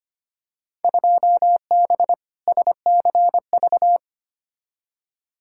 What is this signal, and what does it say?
Morse code, heard as a keyed tone, about 25 wpm, 700 Hz. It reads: 2B SCV
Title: Field Day Exchanges Encore - 25wpm